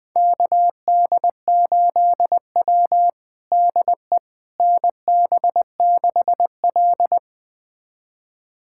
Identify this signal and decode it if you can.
Morse code, heard as a keyed tone, about 21 wpm, 700 Hz. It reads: KD8W DE NB6L
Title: USA Callsign DE Callsign - 4 Characters - 20wpm